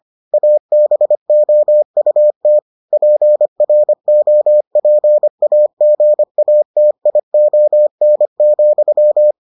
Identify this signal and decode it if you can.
Morse code, heard as a keyed tone, about 25 wpm, 600 Hz. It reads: ABOUT PROPAGATION,